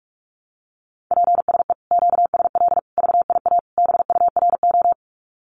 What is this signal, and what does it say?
Morse code, heard as a keyed tone, about 46 wpm, 700 Hz. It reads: PSE QSL VIA BURO
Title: Common QSO Phrases Encore 45wpm